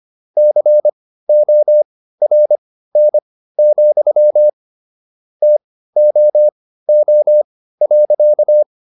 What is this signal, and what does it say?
Morse code, heard as a keyed tone, about 25 wpm, spaced as for 15 wpm, 600 Hz. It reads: CORN, TOO.